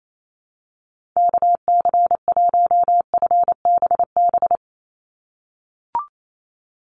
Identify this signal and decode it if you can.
Morse code, heard as a keyed tone, about 28 wpm, 700 Hz. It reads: KC1FBB
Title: USA Call Signs 28wpm